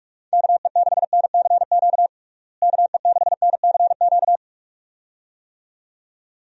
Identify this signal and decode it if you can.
Morse code, heard as a keyed tone, about 46 wpm, 700 Hz. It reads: KE6NCQ KE6NCQ
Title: USA Call Signs - 6 Characters - Repeated 2x - 45wpm